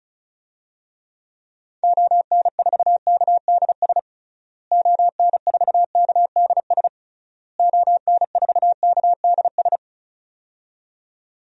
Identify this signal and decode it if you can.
Morse code, heard as a keyed tone, about 36 wpm, 700 Hz. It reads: ON4KDS ON4KDS ON4KDS
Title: DXCC-100 Call Signs - 6 Characters - Repeated 3x - 35wpm